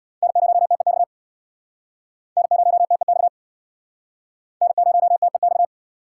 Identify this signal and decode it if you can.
Morse code, heard as a keyed tone, about 66 wpm, 700 Hz. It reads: N0TEX N0TEX N0TEX